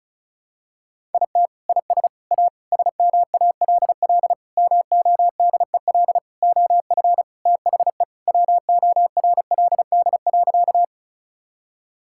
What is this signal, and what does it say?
Morse code, heard as a keyed tone, about 36 wpm, 700 Hz. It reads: IT IS A SMALL MODEL OF THE WORLD.
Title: Sentences from Top 1000 Words - Review Part 5 - 35wpm